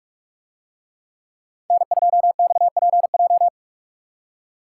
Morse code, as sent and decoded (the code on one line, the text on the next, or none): -. ..--- -..- .--. .---
N2XPJ